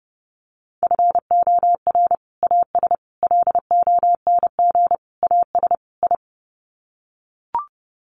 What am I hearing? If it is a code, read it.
Morse code, heard as a keyed tone, about 30 wpm, 700 Hz. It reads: FOR AS LONG AS I
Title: Sets of 5 Words Encore2 30wpm